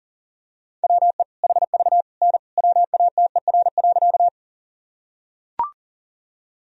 Morse code, heard as a keyed tone, about 40 wpm, 700 Hz. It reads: WE HV N WATER.
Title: Sentences from Top 100 Words and Abbreviations 40wpm